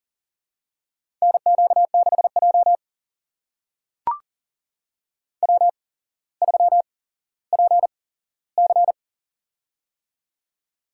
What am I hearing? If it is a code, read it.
Morse code, heard as a keyed tone, about 39 wpm, spaced as for 17 wpm, 700 Hz. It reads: NQ6J W3PC
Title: USA Call Signs - 4 Characters - 8x Character Spacing - 40wpm